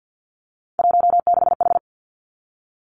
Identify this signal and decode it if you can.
Morse code, heard as a keyed tone, about 50 wpm, 700 Hz. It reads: 16H
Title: Three Challenging Letter-Number Combinations 50wpm